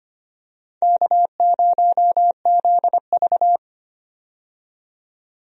Morse code, heard as a keyed tone, about 25 wpm, 700 Hz. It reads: K0ZV
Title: USA Call Signs - 4 Characters - 25wpm